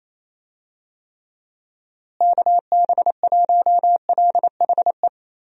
Morse code, heard as a keyed tone, about 27 wpm, 700 Hz. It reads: KB1LHE
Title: USA Call Signs - 6 Characters - Repeated 2x - 28wpm